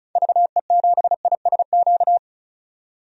Morse code, heard as a keyed tone, about 34 wpm, 700 Hz. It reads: VE7ISQ
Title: DXCC-100 Call Signs - 6 Characters - 35wpm